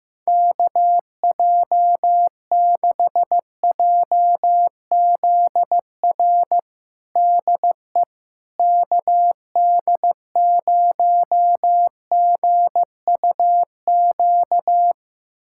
Morse code, heard as a keyed tone, about 15 wpm, 700 Hz. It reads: KJ6JZR DE KD0GUQ
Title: USA Callsign DE Callsign 15wpm